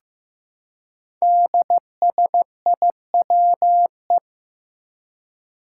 Morse code, heard as a keyed tone, about 15 wpm, 700 Hz. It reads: DSIWE
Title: Five Challenging Letter-Number Combinations 15wpm